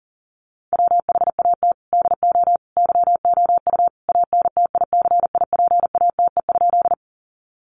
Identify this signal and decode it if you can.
Morse code, heard as a keyed tone, about 40 wpm, 700 Hz. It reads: WHAT DO YOU ANTICIPATE?
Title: Sentences from Top 1500 Words 40wpm